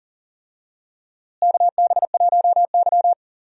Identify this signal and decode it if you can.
Morse code, heard as a keyed tone, about 40 wpm, 700 Hz. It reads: KB1Y
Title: USA Call Signs - 4 Characters - Repeated 3x - 40wpm